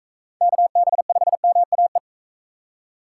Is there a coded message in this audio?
Yes: Morse code from 0.4 to 2.0 s, heard as a keyed tone, about 37 wpm, 700 Hz. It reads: KB5MAE